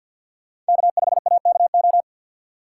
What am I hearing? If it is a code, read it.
Morse code, heard as a keyed tone, about 51 wpm, 690 Hz. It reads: K5AKO
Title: USA Call Signs - 5 Characters - 4x Character Spacing - 50wpm